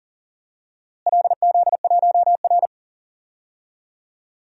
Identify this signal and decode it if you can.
Morse code, heard as a keyed tone, about 40 wpm, 700 Hz. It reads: LZ1R
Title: DXCC-100 Call Signs - 4 Characters - Repeated 2x - 40wpm